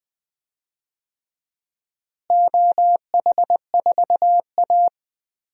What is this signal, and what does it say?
Morse code, heard as a keyed tone, about 20 wpm, 700 Hz. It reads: OH4A
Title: DXCC-100 Call Signs - 4 Characters - Repeated 3x - 20wpm